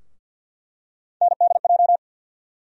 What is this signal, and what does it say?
Morse code, heard as a keyed tone, about 48 wpm, 700 Hz. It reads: NDJ
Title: USA Call Sign Postfixes - 3 Letters 50wpm